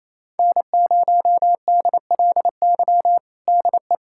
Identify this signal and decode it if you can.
Morse code, heard as a keyed tone, about 28 wpm, 700 Hz. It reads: N0DLY DE
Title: USA Callsign DE Callsign - 5 Characters - 28wpm